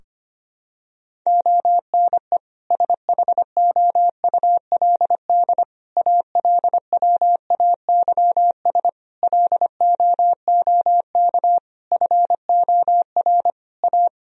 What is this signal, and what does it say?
Morse code, heard as a keyed tone, about 25 wpm, 700 Hz. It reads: ONE SHOULD ALWAYS LOOK FOR A
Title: Sentences from Top 1000 Words - Review Part 4 - 25wpm